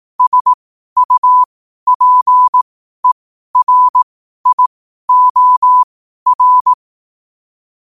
Morse code, heard as a keyed tone, about 18 wpm, spaced as for 12 wpm, 1000 Hz. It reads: SUPERIOR
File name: qotdfiles / 12wpm